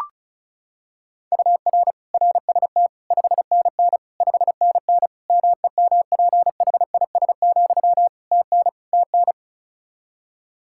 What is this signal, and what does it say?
Morse code, heard as a keyed tone, about 36 wpm, 700 Hz. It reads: UR RST 5NN 5NN MEMPHIS, TN TN
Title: Practice QSO - Exchange 1 - Beginning - 35wpm